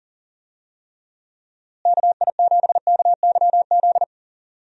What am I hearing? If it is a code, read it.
Morse code, heard as a keyed tone, about 40 wpm, 700 Hz. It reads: KI7KYZ
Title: USA Call Signs - 6 Characters - Repeated 2x - 40wpm